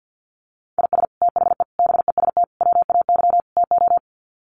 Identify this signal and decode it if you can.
Morse code, heard as a keyed tone, about 49 wpm, 700 Hz. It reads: IS THE BEST WAY TO
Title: Sets of 5 Words Encore 50wpm